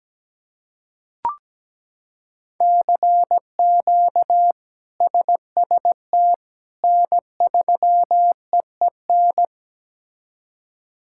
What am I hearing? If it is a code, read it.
Morse code, heard as a keyed tone, about 17 wpm, 700 Hz. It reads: CQ SST N3EEN
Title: K1USN SST QSO Practice - 17wpm